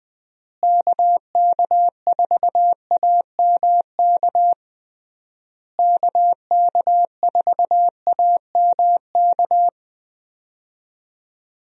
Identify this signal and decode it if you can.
Morse code, heard as a keyed tone, about 21 wpm, 700 Hz. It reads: KK4AMK KK4AMK
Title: USA Call Signs - 6 Characters - Repeated 3x - 20wpm